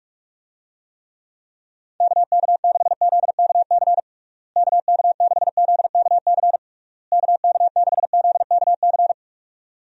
Morse code, heard as a keyed tone, about 46 wpm, 700 Hz. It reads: KK6ZKC KK6ZKC KK6ZKC
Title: USA Call Signs - 6 Characters - Repeated 3x - 45wpm